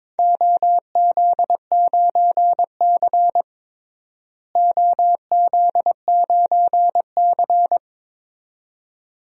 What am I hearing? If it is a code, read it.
Morse code, heard as a keyed tone, about 22 wpm, 700 Hz. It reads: OZ9C OZ9C
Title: DXCC-100 Call Signs - 4 Characters - Repeated 2x - 22wpm